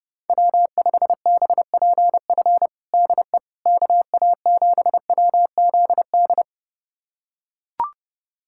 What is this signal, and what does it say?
Morse code, heard as a keyed tone, about 30 wpm, 700 Hz. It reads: W5BPF DE KA7WZD
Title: USA Callsign DE Callsign 30wpm